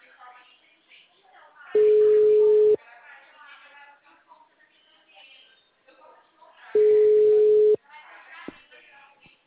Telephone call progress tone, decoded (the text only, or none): ringback tone